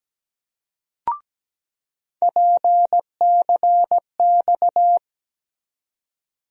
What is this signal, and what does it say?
Morse code, heard as a keyed tone, about 17 wpm, 700 Hz. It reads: PCX
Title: USA Call Sign Postfixes - 3 Letters 17wpm